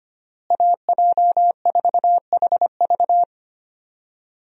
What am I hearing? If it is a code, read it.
Morse code, heard as a keyed tone, about 25 wpm, 700 Hz. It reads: AJ4HV
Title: USA Call Signs - 5 Characters - Repeated 3x - 25wpm